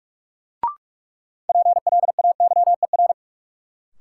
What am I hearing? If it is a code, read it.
Morse code, heard as a keyed tone, about 46 wpm, 700 Hz. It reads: PLAYER
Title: Sentences from Top 1000 Words - Review Part 4 - New Word Drill - 45wpm